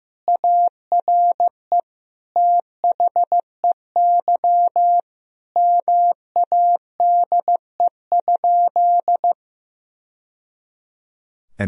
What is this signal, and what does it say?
Morse code, heard as a keyed tone, about 15 wpm, 700 Hz. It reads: ARE THEY MADE?
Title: Sentences from Top 1000 Words - Review Part 3 - 15wpm